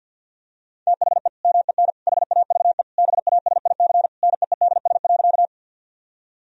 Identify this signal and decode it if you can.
Morse code, heard as a keyed tone, about 50 wpm, 700 Hz. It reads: THE MEN HAVE BASIC NEEDS.
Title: Sentences from Top 1000 Words - Review Part 3 - 50wpm